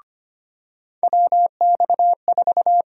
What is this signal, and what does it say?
Morse code, heard as a keyed tone, about 25 wpm, 700 Hz. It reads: WX4
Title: USA Call Sign Prefixes - 25wpm